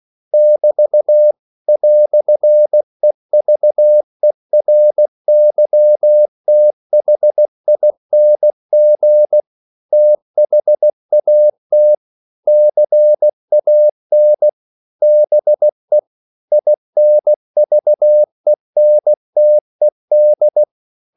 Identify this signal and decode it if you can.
Morse code, heard as a keyed tone, about 16 wpm, 600 Hz. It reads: <BT> "EVERYTHING THAT CAN BE INVENTED
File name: CW / Practice / EnglishQuotations0416